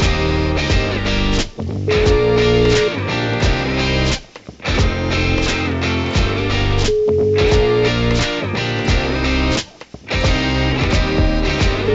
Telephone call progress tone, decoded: ringback tone

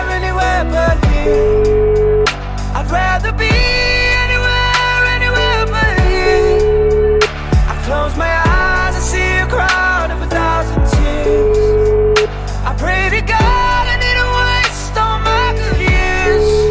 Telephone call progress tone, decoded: ringback tone

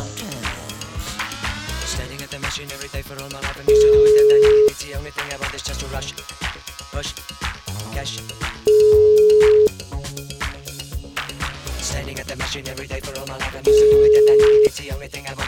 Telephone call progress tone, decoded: ringback tone

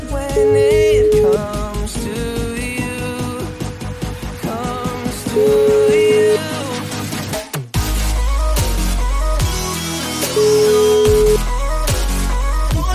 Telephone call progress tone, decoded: ringback tone

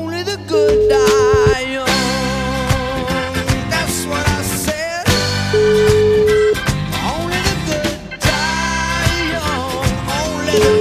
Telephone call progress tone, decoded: ringback tone

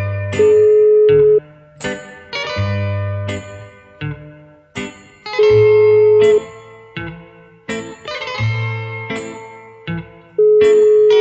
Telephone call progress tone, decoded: ringback tone

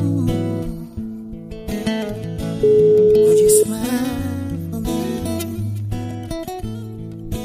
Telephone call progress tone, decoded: ringback tone

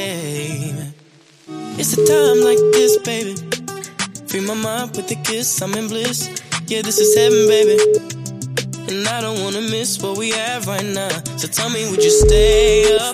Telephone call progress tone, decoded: ringback tone

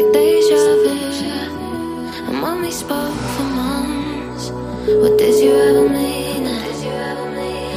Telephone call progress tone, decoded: ringback tone

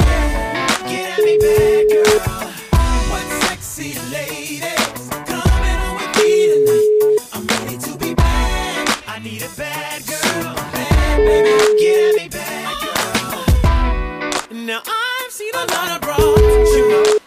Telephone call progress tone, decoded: ringback tone

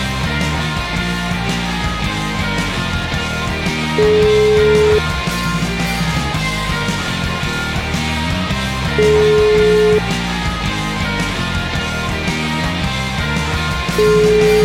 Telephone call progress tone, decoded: ringback tone

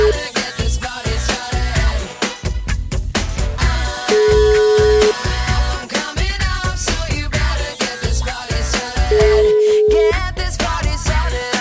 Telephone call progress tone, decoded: ringback tone

